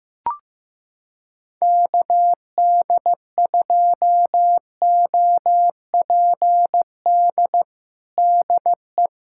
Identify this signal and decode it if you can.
Morse code, heard as a keyed tone, about 15 wpm, 700 Hz. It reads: KD2OPD DE
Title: USA Callsign DE Callsign 15wpm